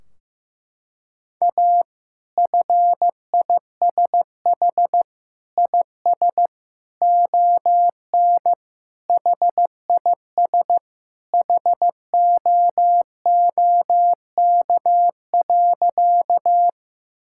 Morse code, heard as a keyed tone, about 15 wpm, 700 Hz. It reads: A FISH IS ON HIS HOOK.
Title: Sentences from Top 1600 Words 15wpm